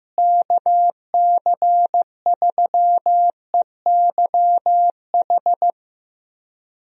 Morse code, heard as a keyed tone, about 15 wpm, 700 Hz. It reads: KC3EYH
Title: USA Call Signs - 6 Characters - Repeated 2x - 15wpm